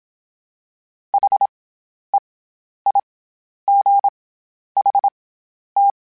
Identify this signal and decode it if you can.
Morse code, heard as a keyed tone, about 27 wpm, spaced as for 11 wpm, 800 Hz. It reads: HEIGHT
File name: Special INGs-mp3s / Special INGs-mp3s / SlowToFaster ing @25